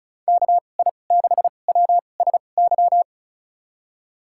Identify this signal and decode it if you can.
Morse code, heard as a keyed tone, about 36 wpm, spaced as for 25 wpm, 700 Hz. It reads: KI6WSY
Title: USA Call Signs - 6 Characters - 2x Character Spacing - 35wpm